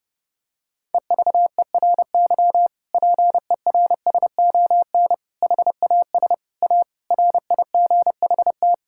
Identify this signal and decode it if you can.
Morse code, heard as a keyed tone, about 32 wpm, 700 Hz. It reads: EVERY PERSON HAS A RIGHT
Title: Sentences from Top 1000 Words - Review Part 5 - 30wpm